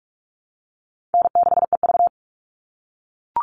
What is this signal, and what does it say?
Morse code, heard as a keyed tone, about 44 wpm, 700 Hz. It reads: N6EV
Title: Call Signs - 45wpm